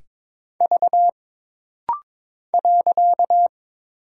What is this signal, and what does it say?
Morse code, heard as a keyed tone, about 22 wpm, 700 Hz. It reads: V .
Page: This rapid-fire timing is geared to push your ICR (Instant Character Recognition) skills.